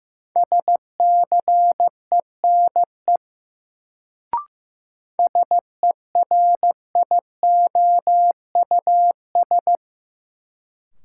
Morse code, heard as a keyed tone, about 15 wpm, 700 Hz. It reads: SCENE SERIOUS
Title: Sentences from Top 1100 Words - New Word Drill - 15wpm